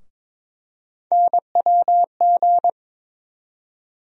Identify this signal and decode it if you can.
Morse code, heard as a keyed tone, about 22 wpm, 700 Hz. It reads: NWG
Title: USA Call Sign Postfixes - 3 Letters 22wpm